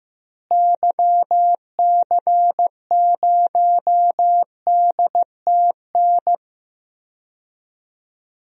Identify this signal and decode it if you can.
Morse code, heard as a keyed tone, about 15 wpm, 700 Hz. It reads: YC0DTN
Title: DXCC-100 Call Signs - 6 Characters - Repeated 2x - 15wpm